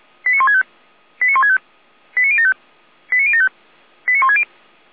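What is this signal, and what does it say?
Five-tone selective call (ZVEI), heard as quick runs of five tones, tones of about 80 ms. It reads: 89165 89165 89075 89075 89160
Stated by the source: ZVEI five-tone selective call 89099, SDR recording, Feuerwehr Orlamünde.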